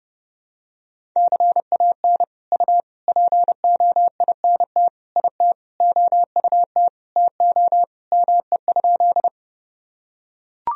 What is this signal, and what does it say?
Morse code, heard as a keyed tone, about 32 wpm, 700 Hz. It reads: CAN U POINT IT OUT TO ME?
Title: Sentences from Top 200 Words and Abbreviations 30wpm